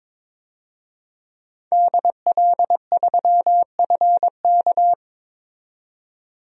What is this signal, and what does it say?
Morse code, heard as a keyed tone, about 22 wpm, 700 Hz. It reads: DL3FK
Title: DXCC-100 Call Signs - 5 Characters - 22wpm